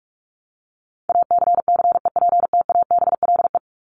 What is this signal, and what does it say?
Morse code, heard as a keyed tone, about 44 wpm, 700 Hz. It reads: ACCEPTABLE
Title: Top 1501-1750 Words 45wpm